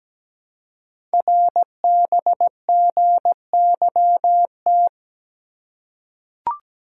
Morse code, heard as a keyed tone, about 17 wpm, 700 Hz. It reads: RBGYT
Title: Code Groups - 5 Letters - 17wpm